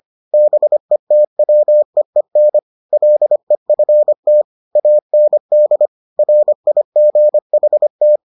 Morse code, heard as a keyed tone, about 25 wpm, 600 Hz. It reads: BETWEEN LEFT AND RIGHT